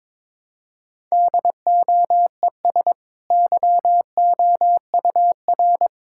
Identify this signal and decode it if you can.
Morse code, heard as a keyed tone, about 22 wpm, 700 Hz. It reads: DOES YOUR
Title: Sentences from Top 1000 Words - Review Part 6 - 22wpm